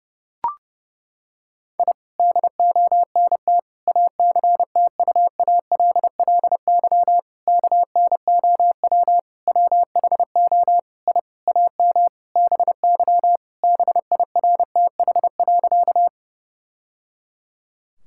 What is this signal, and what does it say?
Morse code, heard as a keyed tone, about 30 wpm, 700 Hz. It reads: I DONT ACTUALLY KNOW WHO I AM BY BIRTH.
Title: Sentences from Top 1000 Words - Review Part 5 - 30wpm